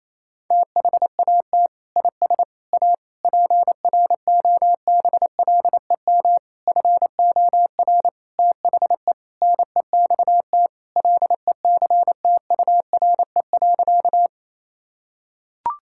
Morse code, heard as a keyed tone, about 28 wpm, 700 Hz. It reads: THAT IS A PROBLEM FOR THE NEXT LECTURE.
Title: Sentences from Top 1600 Words 28wpm